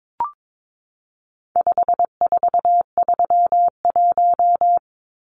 Morse code, heard as a keyed tone, about 22 wpm, 700 Hz. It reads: 5431
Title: Numbers - 4-Digit 22wpm